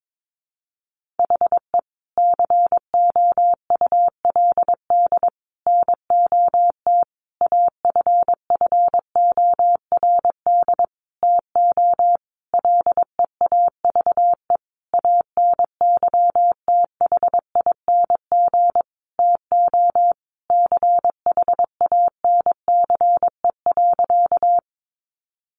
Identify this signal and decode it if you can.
Morse code, heard as a keyed tone, about 22 wpm, 700 Hz. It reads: HE COULD NOT AFFORD TO LEAVE ANYTHING TO CHANCE.
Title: Sentences from Top 1100 Words 22wpm